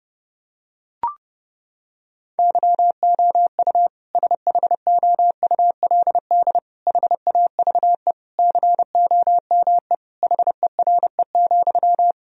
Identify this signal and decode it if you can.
Morse code, heard as a keyed tone, about 30 wpm, 700 Hz. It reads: YOU SHOULD HAVE COME HERE,
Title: Sentences from Top 1000 Words - Review Part 4 - 30wpm